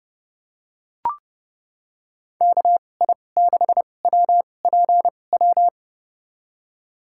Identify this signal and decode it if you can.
Morse code, heard as a keyed tone, about 31 wpm, spaced as for 22 wpm, 700 Hz. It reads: KI6WPW
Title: USA Call Signs - 2x Character Spacing - 30wpm